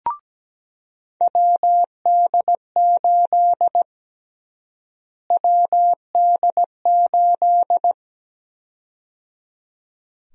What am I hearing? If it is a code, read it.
Morse code, heard as a keyed tone, about 17 wpm, 700 Hz. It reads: WD8 WD8
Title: USA Call Sign Prefixes - Repeated 2x - 17wpm